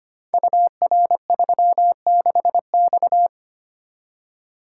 Morse code, heard as a keyed tone, about 25 wpm, 690 Hz. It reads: UR36X
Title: Code Groups - 5 Letter:Numbers - 25wpm